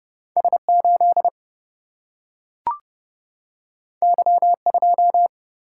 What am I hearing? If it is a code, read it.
Morse code, heard as a keyed tone, about 30 wpm, 700 Hz. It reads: S8 Y2
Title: Single-Letter Single-Number 30wpm